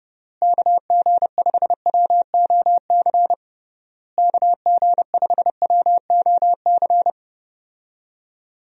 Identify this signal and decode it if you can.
Morse code, heard as a keyed tone, about 30 wpm, 700 Hz. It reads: KG5WOC KG5WOC
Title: USA Call Signs - 6 Characters - Repeated 3x - 30wpm